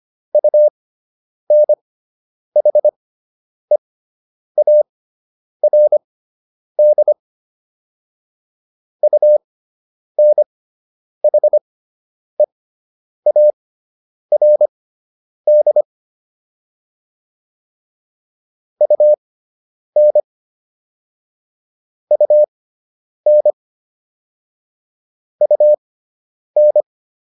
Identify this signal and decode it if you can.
Morse code, heard as a keyed tone, about 25 wpm, spaced as for 9 wpm, 600 Hz. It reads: UNHEARD UNHEARD UN UN UN